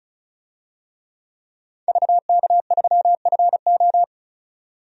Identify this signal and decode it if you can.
Morse code, heard as a keyed tone, about 34 wpm, 700 Hz. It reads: VK3FO
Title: DXCC-100 Call Signs - 5 Characters - 35wpm